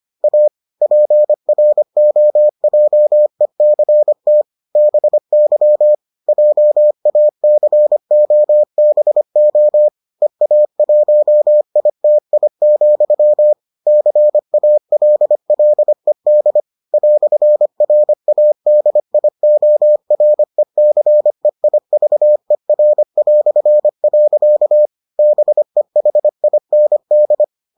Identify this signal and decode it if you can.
Morse code, heard as a keyed tone, about 25 wpm, 600 Hz. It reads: A PROJECT BY JACOBO EA1ITI, CALLED "RADIORECEIVER". BEHIND